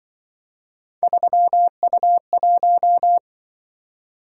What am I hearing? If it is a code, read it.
Morse code, heard as a keyed tone, about 24 wpm, 700 Hz. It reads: 3U1
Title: Speed-Racing - Three Letter-Number Combinations - 20wpm